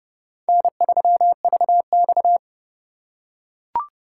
Morse code, heard as a keyed tone, about 30 wpm, 700 Hz. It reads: N3VX